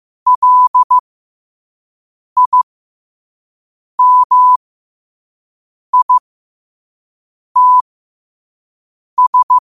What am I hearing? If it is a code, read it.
Morse code, heard as a keyed tone, about 15 wpm, spaced as for 5 wpm, 1000 Hz. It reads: LIMITS
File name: qotdfiles / 5wpm